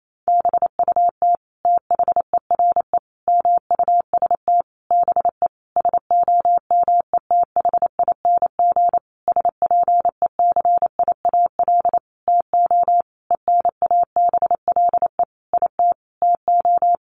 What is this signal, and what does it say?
Morse code, heard as a keyed tone, about 28 wpm, 700 Hz. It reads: BUT THERE MUST BE SOMETHING SPECIAL TO ENABLE IT TO
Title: Sentences from Top 1500 Words 28wpm